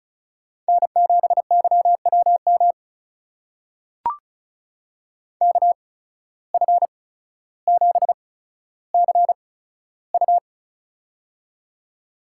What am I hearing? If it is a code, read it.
Morse code, heard as a keyed tone, about 34 wpm, spaced as for 15 wpm, 700 Hz. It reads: N7YWM KF7CU
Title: USA Call Signs - 5 Characters - 8x Character Spacing - 35wpm